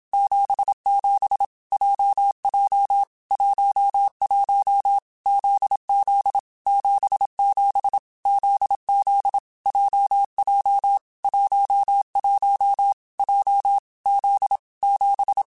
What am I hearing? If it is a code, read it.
Morse code, heard as a keyed tone, about 27 wpm, 800 Hz. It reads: 77 JJ 11 ZZ 77 ZZ JJ 11 J Z 7